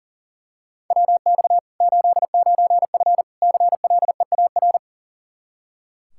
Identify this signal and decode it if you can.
Morse code, heard as a keyed tone, about 40 wpm, 700 Hz. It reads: WX 89F CLEAR
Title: Practice QSO - Exchange 2 - Info - 40wpm